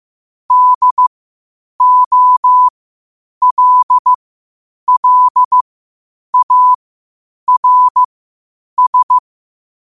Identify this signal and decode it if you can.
Morse code, heard as a keyed tone, about 15 wpm, spaced as for 8 wpm, 1000 Hz. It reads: DOLLARS
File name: qotdfiles / 8wpm